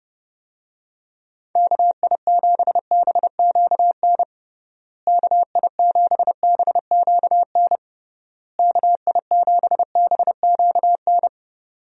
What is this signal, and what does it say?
Morse code, heard as a keyed tone, about 30 wpm, 700 Hz. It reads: KI7BQN KI7BQN KI7BQN
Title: USA Call Signs - 6 Characters - Repeated 3x - 30wpm